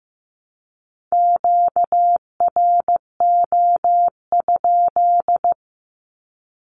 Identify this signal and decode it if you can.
Morse code, heard as a keyed tone, about 15 wpm, 700 Hz. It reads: QRO?
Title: Q-Codes Repeated 3x 15wpm